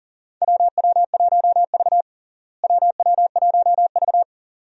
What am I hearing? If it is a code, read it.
Morse code, heard as a keyed tone, about 37 wpm, 700 Hz. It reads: WW1V WW1V
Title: USA Call Signs - 4 Characters - Repeated 2x - 40wpm